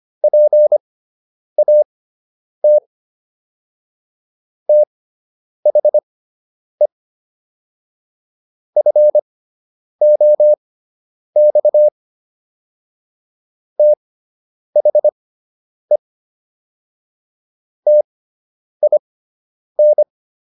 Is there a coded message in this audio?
Yes: Morse code from 0.2 to 20.0 s, heard as a keyed tone, about 25 wpm, spaced as for 9 wpm, 600 Hz. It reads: PAT THE FOX THE TIN